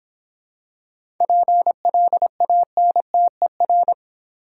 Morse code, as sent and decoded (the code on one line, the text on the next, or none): .--. .-.. .- -. - . .-.
PLANTER